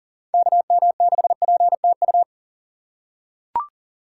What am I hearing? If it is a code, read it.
Morse code, heard as a keyed tone, about 38 wpm, 700 Hz. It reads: KM6PTU
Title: USA Call Signs - 6x Character Spacing - 40wpm